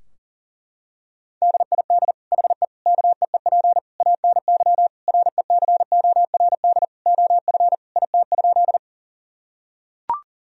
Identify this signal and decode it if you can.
Morse code, heard as a keyed tone, about 37 wpm, 700 Hz. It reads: DID HE KEEP ANY RECORD OF IT?
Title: Sentences from Top 1000 Words - Review Part 3 - 40wpm